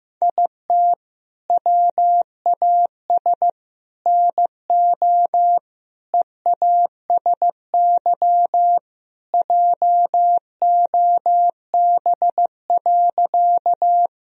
Morse code, heard as a keyed tone, about 16 wpm, 700 Hz. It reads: IT WAS NO EASY JOB.
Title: Sentences from Top 1000 Words - Review Part 3 - 15wpm